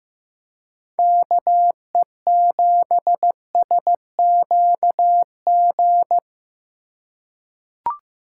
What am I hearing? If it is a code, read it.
Morse code, heard as a keyed tone, about 15 wpm, 700 Hz. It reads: KE7SQG
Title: USA Call Signs - 3x Character Spacing - 15wpm